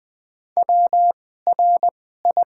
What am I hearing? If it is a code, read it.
Morse code, heard as a keyed tone, about 20 wpm, spaced as for 15 wpm, 700 Hz. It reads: WRI
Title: 2x Character Spacing - Top 100 Words - 20wpm